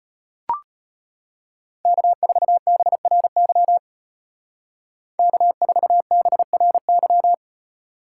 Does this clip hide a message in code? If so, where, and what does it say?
Yes: Morse code from 1.9 to 7.3 s, heard as a keyed tone, about 33 wpm, 700 Hz. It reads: K4BRY K4BRY